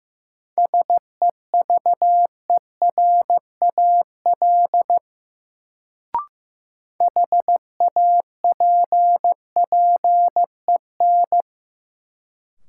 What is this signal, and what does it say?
Morse code, heard as a keyed tone, about 15 wpm, 700 Hz. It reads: SEVERAL HAPPEN
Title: Sentences from Top 400 Words - New Word Drill - 15wpm